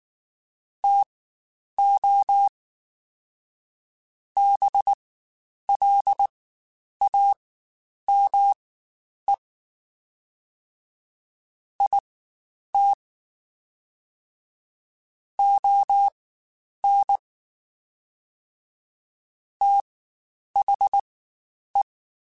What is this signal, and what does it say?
Morse code, heard as a keyed tone, about 19 wpm, spaced as for 9 wpm, 800 Hz. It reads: TO BLAME IT ON THE
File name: Spaced / Chuckles Spaced-mp3s / HumanErr sp @18